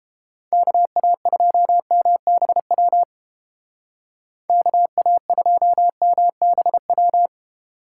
Morse code, heard as a keyed tone, about 29 wpm, 700 Hz. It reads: KA2MBW KA2MBW